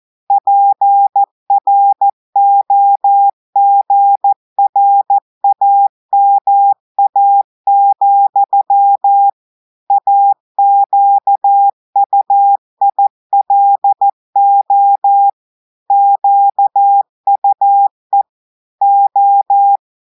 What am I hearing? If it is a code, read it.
Morse code, heard as a keyed tone, about 14 wpm, 800 Hz. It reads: PROGRAMA, AQUILO QUE O
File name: audios / portugais / portugais texte en clair 2 vitesse 14 mots minutes